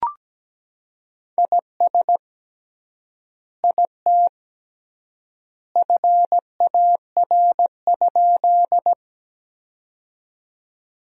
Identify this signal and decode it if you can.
Morse code, heard as a keyed tone, about 17 wpm, 700 Hz. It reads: IS IT FAR?
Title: 3x Word-Spacing - Sentences from Top 300 Words - 17wpm